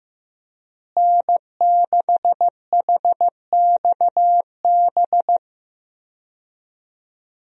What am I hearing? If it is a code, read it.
Morse code, heard as a keyed tone, about 15 wpm, 700 Hz. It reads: N6HXB
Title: USA Call Signs - 5 Characters - Repeated 3x - 15wpm